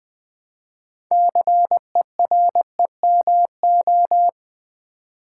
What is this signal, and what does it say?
Morse code, heard as a keyed tone, about 20 wpm, 700 Hz. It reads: CEREMO
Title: Build A Word - 3 Letters At A Time - 8-Letter Words - 20wpm